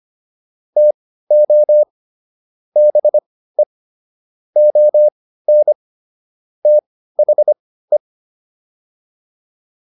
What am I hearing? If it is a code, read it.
Morse code, heard as a keyed tone, about 25 wpm, spaced as for 15 wpm, 600 Hz. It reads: TO BE ON THE
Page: intermediate-practice-files